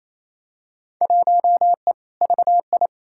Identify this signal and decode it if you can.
Morse code, heard as a keyed tone, about 28 wpm, 700 Hz. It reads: 1E VI
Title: Field Day Exchanges Encore - 28wpm